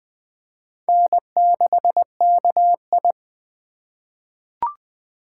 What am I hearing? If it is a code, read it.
Morse code, heard as a keyed tone, about 21 wpm, 700 Hz. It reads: N6KI
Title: USA Call Signs - 4 Characters - 4x Character Spacing - 20wpm